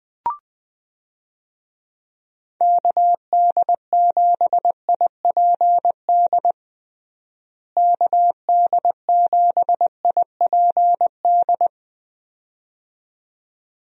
Morse code, heard as a keyed tone, about 21 wpm, 700 Hz. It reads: KD7IPD KD7IPD
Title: USA Call Signs - 6 Characters - Repeated 2x - 20wpm